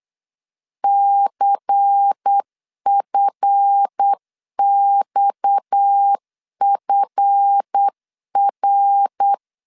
Morse code, heard as a keyed tone, about 8 wpm, 800 Hz. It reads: CFXFR
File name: LEZIONI / 7 GF9Z / 7 GF 8wpm 3108